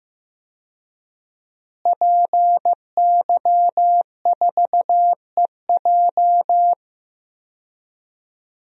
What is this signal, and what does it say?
Morse code, heard as a keyed tone, about 15 wpm, 700 Hz. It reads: PY4EJ
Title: DXCC-100 Call Signs - 5 Characters - Repeated 2x - 15wpm